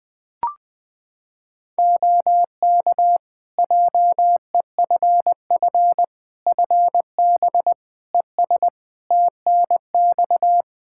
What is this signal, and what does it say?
Morse code, heard as a keyed tone, about 20 wpm, 700 Hz. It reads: OK JEFF FB ES TNX
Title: K1USN SST QSO Practice 20wpm